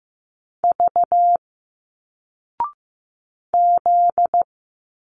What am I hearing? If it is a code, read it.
Morse code, heard as a keyed tone, about 15 wpm, 700 Hz. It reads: V Z
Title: Single Letters 15wpm